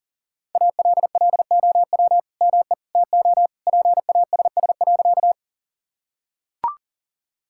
Morse code, heard as a keyed tone, about 40 wpm, 690 Hz. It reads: ALLOW ME TO PASS.
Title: Sentences from Top 1000 Words - Review Part 5 - 40wpm